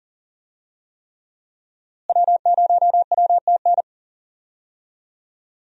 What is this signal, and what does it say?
Morse code, heard as a keyed tone, about 40 wpm, 700 Hz. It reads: W0WTN
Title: USA Call Signs - 5 Characters - Repeated 3x - 40wpm